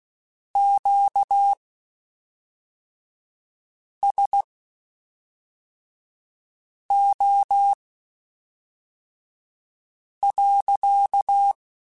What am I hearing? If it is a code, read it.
Morse code, heard as a keyed tone, about 16 wpm, spaced as for 3 wpm, 800 Hz. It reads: QSO.
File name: Q Calls / Q Calls-mp3s / Q Calls sppc @5